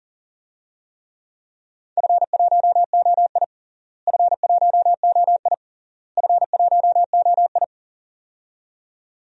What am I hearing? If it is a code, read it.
Morse code, heard as a keyed tone, about 40 wpm, 700 Hz. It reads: F1OI F1OI F1OI
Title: DXCC-100 Call Signs - 4 Characters - Repeated 3x - 40wpm